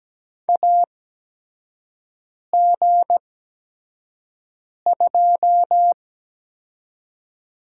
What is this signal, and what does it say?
Morse code, heard as a keyed tone, about 17 wpm, spaced as for 5 wpm, 700 Hz. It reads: AG2
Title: USA Call Signs - 8x Character Spacing - 17wpm